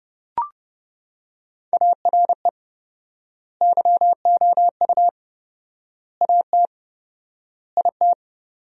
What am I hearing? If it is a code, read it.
Morse code, heard as a keyed tone, about 32 wpm, 700 Hz. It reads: ARE YOU AT IT
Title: 4x Word-Spacing - Sentences from Top 200 Words - 30wpm